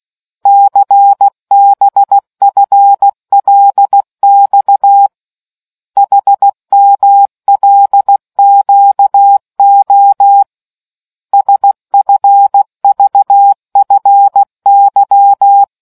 Morse code, heard as a keyed tone, about 16 wpm, 800 Hz. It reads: CBFLX HMLQO SFVFY